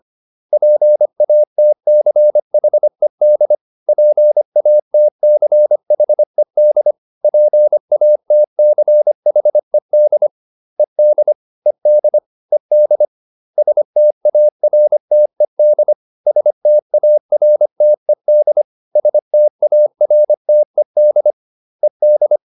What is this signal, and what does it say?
Morse code, heard as a keyed tone, about 25 wpm, 600 Hz. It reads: PATCHED PATCHED PATCHED ED ED ED STARTED STARTED STARTED ED